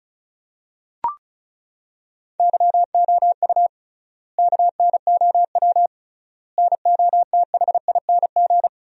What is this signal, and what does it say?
Morse code, heard as a keyed tone, about 36 wpm, 700 Hz. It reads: YOU KNOW NOTHING
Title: 3x Word-Spacing - Sentences from Top 500 Words - 35wpm